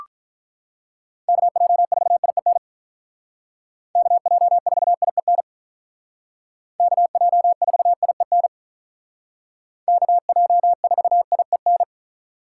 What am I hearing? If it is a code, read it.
Morse code, heard as a keyed tone, about 43 wpm, 700 Hz. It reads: KJ4IEN KJ4IEN KJ4IEN KJ4IEN
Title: Speed Racing - USA 2x3 Call Signs 35wpm